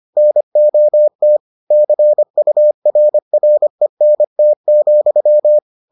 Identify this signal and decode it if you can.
Morse code, heard as a keyed tone, about 25 wpm, 600 Hz. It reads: NOT CURRENT,